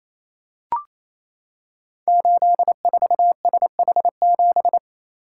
Morse code, heard as a keyed tone, about 27 wpm, 700 Hz. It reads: 84SH7
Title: Code Groups - 5 Letter:Numbers - 28wpm